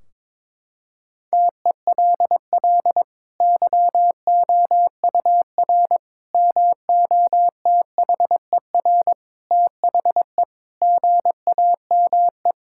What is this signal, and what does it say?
Morse code, heard as a keyed tone, about 21 wpm, 700 Hz. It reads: TELL YOUR MOTHER THE GAME
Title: Sentences from Top 1600 Words 22wpm